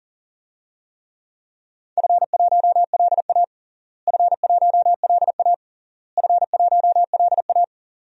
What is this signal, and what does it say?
Morse code, heard as a keyed tone, about 40 wpm, 700 Hz. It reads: F1LA F1LA F1LA
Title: DXCC-100 Call Signs - 4 Characters - Repeated 3x - 40wpm